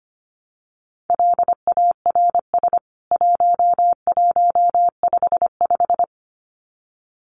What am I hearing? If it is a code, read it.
Morse code, heard as a keyed tone, about 25 wpm, 700 Hz. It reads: LARS 1155
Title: CWT Exchange 25wpm